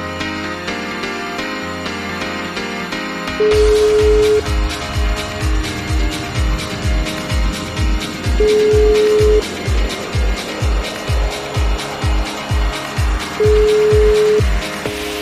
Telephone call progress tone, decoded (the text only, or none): ringback tone